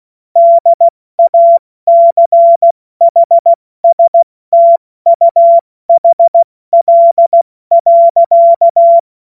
Morse code, heard as a keyed tone, about 16 wpm, 680 Hz. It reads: DACHSTUHL.
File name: Morsekurs / Witze / Witz 01. November 2022 70 F80